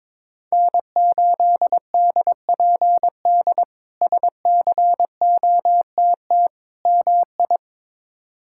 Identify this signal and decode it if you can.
Morse code, heard as a keyed tone, about 22 wpm, 700 Hz. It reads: N8DPD SCOTT MI
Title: K1USN SST QSO Practice - 22wpm